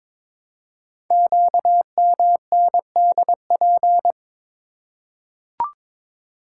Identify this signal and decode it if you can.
Morse code, heard as a keyed tone, about 23 wpm, 700 Hz. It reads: QMNDP
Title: Code Groups - 5 Letters - 22wpm